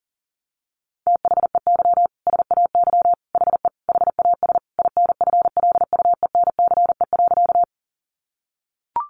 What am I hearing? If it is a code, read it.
Morse code, heard as a keyed tone, about 40 wpm, 700 Hz. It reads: THEY SAY HE HAS INFLUENCE.
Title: Sentences from Top 1100 Words 40wpm